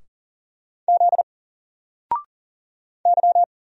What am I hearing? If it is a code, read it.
Morse code, heard as a keyed tone, about 38 wpm, 700 Hz. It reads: Z Y
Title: Single Character (Letter:Number:Punctuation:Prosign) Alt 40wpm